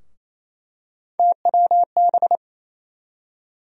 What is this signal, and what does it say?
Morse code, heard as a keyed tone, about 28 wpm, 700 Hz. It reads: TWB
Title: USA Call Sign Postfixes - 3 Letters 28wpm